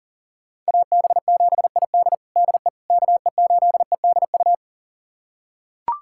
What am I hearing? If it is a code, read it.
Morse code, heard as a keyed tone, about 40 wpm, 700 Hz. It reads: AB7ID DE KE8EDU
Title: USA Callsign DE Callsign 40wpm